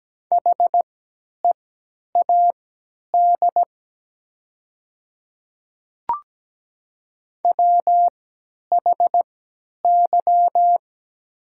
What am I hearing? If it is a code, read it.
Morse code, heard as a keyed tone, about 17 wpm, spaced as for 10 wpm, 700 Hz. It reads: HEAD WHY
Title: 3x Character Spacing - Top 500 Words - 17wpm